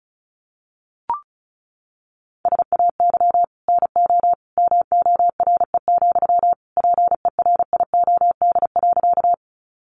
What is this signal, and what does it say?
Morse code, heard as a keyed tone, about 34 wpm, 700 Hz. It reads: SAY NO MORE, PERIOD.
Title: Sentences from Top 800 Words 35wpm